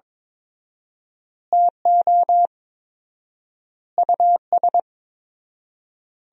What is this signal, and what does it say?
Morse code, heard as a keyed tone, about 22 wpm, 700 Hz. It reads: TO US
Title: 4x Word-Spacing - Sentences from Top 500 Words - 22wpm